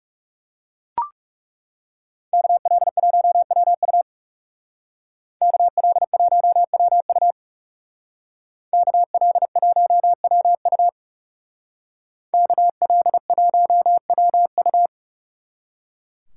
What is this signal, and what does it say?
Morse code, heard as a keyed tone, about 38 wpm, 700 Hz. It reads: KL1WU KL1WU KL1WU KL1WU
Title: Speed Racing - USA 2x2 Call Signs 30wpm